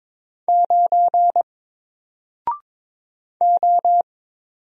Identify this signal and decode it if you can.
Morse code, heard as a keyed tone, about 22 wpm, 700 Hz. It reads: 9 O
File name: Single-Letter-Number-22wpm